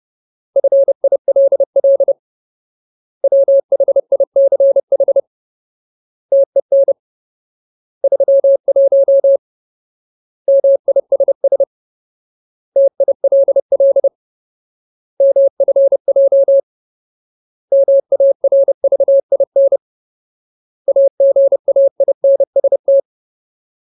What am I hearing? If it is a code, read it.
Morse code, heard as a keyed tone, about 30 wpm, 550 Hz. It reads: FILL WHICH TEN 31 MISS TILL MFJ MARVIN AGAINST
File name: Ultimate Word List / 30WPM Ultimate